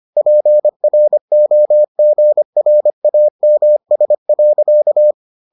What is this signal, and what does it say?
Morse code, heard as a keyed tone, about 25 wpm, 600 Hz. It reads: PROGRAMS.